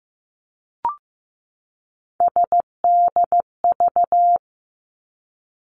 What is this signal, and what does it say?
Morse code, heard as a keyed tone, about 15 wpm, 700 Hz. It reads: SDV
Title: Three Challenging Letter-Number Combinations 15wpm